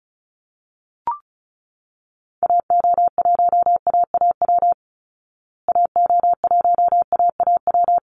Morse code, heard as a keyed tone, about 34 wpm, 700 Hz. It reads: AO1AAW AO1AAW
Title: Call Signs Repeated 3x - 35wpm